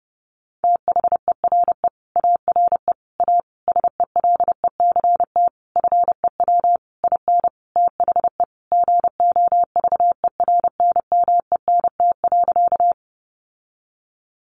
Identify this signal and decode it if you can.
Morse code, heard as a keyed tone, about 30 wpm, 700 Hz. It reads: THERE ARE A SELECT FEW IN THE GOVERNMENT.
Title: Sentences from Top 1300 Words 30wpm